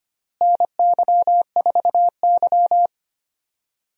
Morse code, heard as a keyed tone, about 25 wpm, 700 Hz. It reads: NY4Y
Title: USA Call Signs - 4 Characters - Repeated 2x - 25wpm